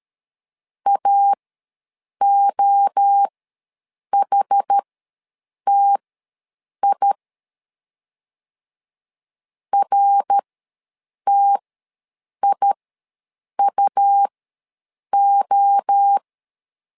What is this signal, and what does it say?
Morse code, heard as a keyed tone, about 13 wpm, spaced as for 7 wpm, 800 Hz. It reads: AOHTI RTIUO
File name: LEZIONI / 3 MWRO / 3 MWRO ws 12wpm 1914